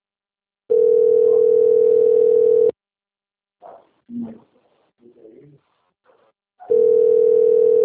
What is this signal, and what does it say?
Ringback tone (North America), 440 + 480 Hz.
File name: rg-1600-unknown- - - .29743